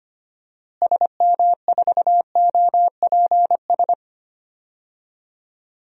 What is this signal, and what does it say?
Morse code, heard as a keyed tone, about 25 wpm, 700 Hz. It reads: SM4OPS
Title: DXCC-100 Call Signs - 6 Characters - Repeated 3x - 25wpm